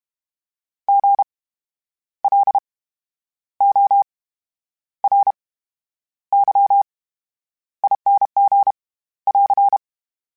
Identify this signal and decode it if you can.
Morse code, heard as a keyed tone, about 32 wpm, 800 Hz. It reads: G L O R Y ING <AR>
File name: Special INGs / Special INGs-mp3s / VerySlowToFaster Ing @30